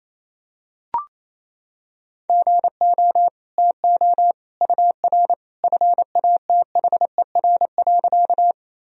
Morse code, heard as a keyed tone, about 28 wpm, 700 Hz. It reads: GO TO UR FATHER.
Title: Sentences from Top 300 Words and Abbreviations 28wpm